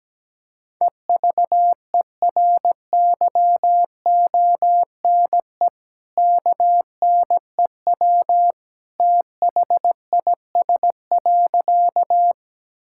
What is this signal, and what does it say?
Morse code, heard as a keyed tone, about 17 wpm, 700 Hz. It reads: EVERYONE KNEW THIS.
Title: Sentences from Top 1000 Words - Review Part 5 - 17wpm